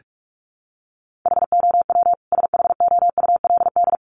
Morse code, heard as a keyed tone, about 44 wpm, 700 Hz. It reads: HOW SHOULD
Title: Sentences from Top 1300 Words 45wpm